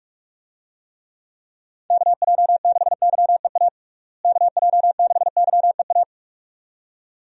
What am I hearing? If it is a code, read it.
Morse code, heard as a keyed tone, about 46 wpm, 700 Hz. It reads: KJ6YEA KJ6YEA
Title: USA Call Signs - 6 Characters - Repeated 2x - 45wpm